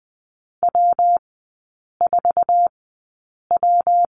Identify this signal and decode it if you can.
Morse code, heard as a keyed tone, about 20 wpm, spaced as for 8 wpm, 700 Hz. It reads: W4W
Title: 114 Lesson 19e - Letter w - Three Characters 20wpm